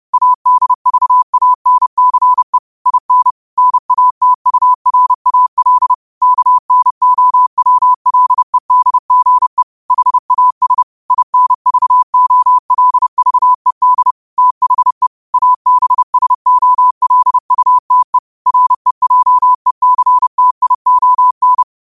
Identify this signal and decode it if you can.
Morse code, heard as a keyed tone, about 30 wpm, 1000 Hz. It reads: ADVANCE IN NATURAL KNOWLEDGE HAS INVOLVED THE ABSOLUTE REJECTION
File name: qotdfiles / 30wpm